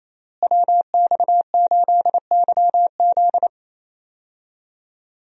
Morse code, heard as a keyed tone, about 28 wpm, 700 Hz. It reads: WX8YZ
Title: USA Call Signs - 5 Characters - Repeated 2x - 28wpm